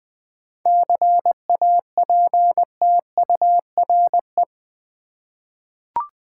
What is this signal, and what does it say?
Morse code, heard as a keyed tone, about 20 wpm, 700 Hz. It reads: CAPTURE
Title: 7 Letter Words - First 3 Letters Given - 20wpm